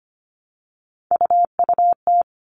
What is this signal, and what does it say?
Morse code, heard as a keyed tone, about 25 wpm, 700 Hz. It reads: UUT
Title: All Cut Numbers - 3-Digit 25wpm